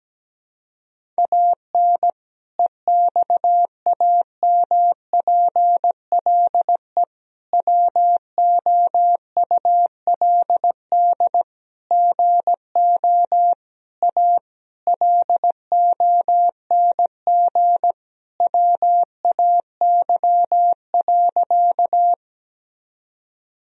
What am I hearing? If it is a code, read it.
Morse code, heard as a keyed tone, about 17 wpm, 700 Hz. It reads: AN EXAMPLE WOULD GO A LONG WAY.
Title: Sentences from Top 1000 Words - Review Part 1 - 17wpm